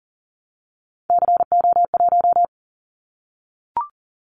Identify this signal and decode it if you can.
Morse code, heard as a keyed tone, about 40 wpm, 700 Hz. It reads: CO1
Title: Callsign Prefixes for Top 100 DXCC - 40wpm